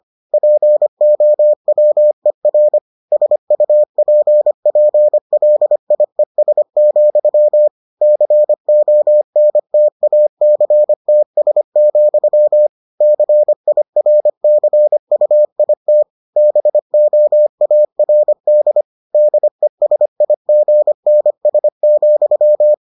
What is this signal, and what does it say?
Morse code, heard as a keyed tone, about 25 wpm, 600 Hz. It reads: POWER SUPPLIES, CONTACTS, CIRCUIT BOARD DESIGNS,